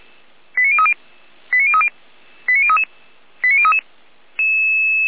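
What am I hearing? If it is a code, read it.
Five-tone selective call (ZVEI), heard as quick runs of five tones, tones of about 70 ms. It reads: 89030 89030 89033 89033
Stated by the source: ZVEI five-tone selective call received on 455 via SDR.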